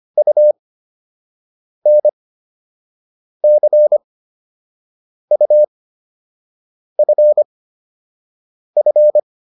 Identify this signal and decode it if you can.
Morse code, heard as a keyed tone, about 25 wpm, spaced as for 6 wpm, 600 Hz. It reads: UNCUFF